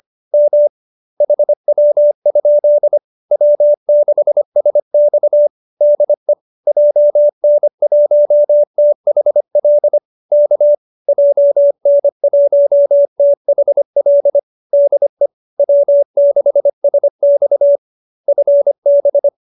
Morse code, heard as a keyed tone, about 25 wpm, 570 Hz. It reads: M HW? W6SX DE JN1THL K JN1THL DE W6SX FB
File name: QSO305 25